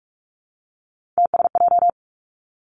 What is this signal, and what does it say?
Morse code, heard as a keyed tone, about 46 wpm, 700 Hz. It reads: TSJ
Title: Three Challenging Letter-Number Combinations 45wpm